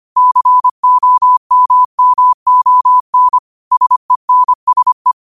Morse code, heard as a keyed tone, about 25 wpm, 1000 Hz. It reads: COMMON SENSE
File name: qotdfiles / 25wpm